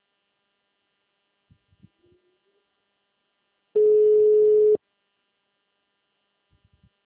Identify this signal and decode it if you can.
Ringback tone (Europe, 425 Hz).
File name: out- -2011- - - .1363